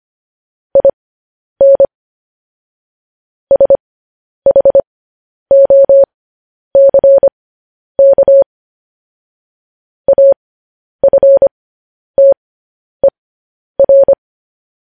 Morse code, heard as a keyed tone, about 25 wpm, spaced as for 10 wpm, 550 Hz. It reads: IN SHOCK AFTER
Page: news-headlines-in-morse-code-at-10-wpm